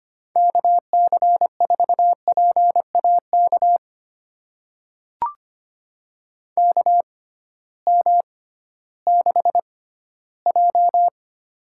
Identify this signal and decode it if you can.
Morse code, heard as a keyed tone, about 25 wpm, spaced as for 15 wpm, 700 Hz. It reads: KC4PAK KM6J
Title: USA Call Signs - 6 Characters - 6x Character Spacing - 25wpm